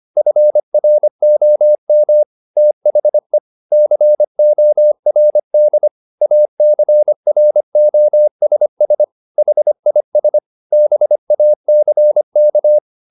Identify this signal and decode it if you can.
Morse code, heard as a keyed tone, about 25 wpm, 600 Hz. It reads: FROM THE CORD ACROSS HIS BACK